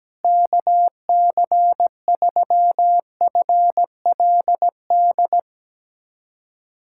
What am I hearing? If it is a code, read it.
Morse code, heard as a keyed tone, about 17 wpm, 700 Hz. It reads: KC3FLD
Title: USA Callsign DE Callsign - 6 Characters - 17wpm